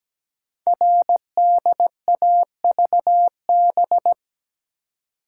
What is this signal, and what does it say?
Morse code, heard as a keyed tone, about 17 wpm, 700 Hz. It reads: RDAVB
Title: Code Groups - 5 Letters - 17wpm